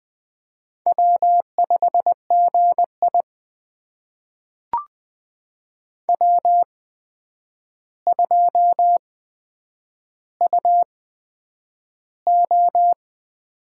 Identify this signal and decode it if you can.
Morse code, heard as a keyed tone, about 20 wpm, spaced as for 9 wpm, 700 Hz. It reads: W5GI W2UO
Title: USA Call Signs - 4 Characters - 8x Character Spacing - 20wpm